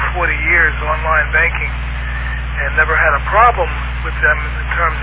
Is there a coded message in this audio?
Yes: Emergency Alert System (SAME) tones from 1.1 to 4.7 s, heard as harsh buzzing bursts, with speech.